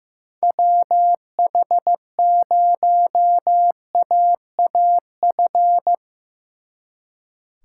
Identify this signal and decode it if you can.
Morse code, heard as a keyed tone, about 17 wpm, 700 Hz. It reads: WH0AAF
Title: Speed Racing - USA Call Signs 15wpm